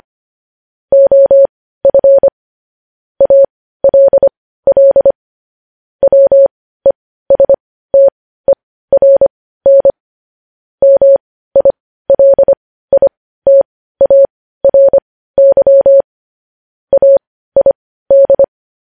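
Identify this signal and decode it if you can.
Morse code, heard as a keyed tone, about 25 wpm, spaced as for 15 wpm, 550 Hz. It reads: OF ALL WESTERN MILITARY AID
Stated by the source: Morse code news headlines sent at 15.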